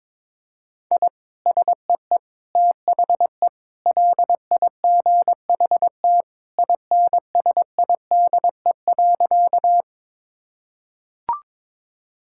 Morse code, heard as a keyed tone, about 22 wpm, 700 Hz. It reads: I SEE THE LIGHT INSIDE.
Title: Sentences from Top 1000 Words - Review Part 6 - 22wpm